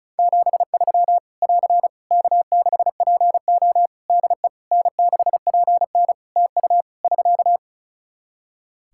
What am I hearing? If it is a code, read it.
Morse code, heard as a keyed tone, about 36 wpm, 700 Hz. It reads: 73 <AR> K6PO DE N6PN TU <SK>
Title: K1USN SST QSO Practice 35wpm